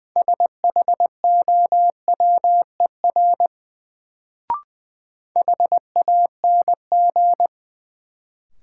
Morse code, heard as a keyed tone, about 20 wpm, 700 Hz. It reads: SHOWER HANG
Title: Sentences from Top 1600 Words - New Word Drill - 20wpm